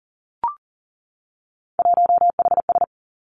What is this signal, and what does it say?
Morse code, heard as a keyed tone, about 39 wpm, 700 Hz. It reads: 1HS
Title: Three Challenging Letter-Number Combinations 40wpm